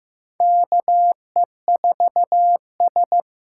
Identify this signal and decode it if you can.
Morse code, heard as a keyed tone, about 15 wpm, 700 Hz. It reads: KE4S
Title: USA Call Signs - 6 Characters - 6x Character Spacing - 15wpm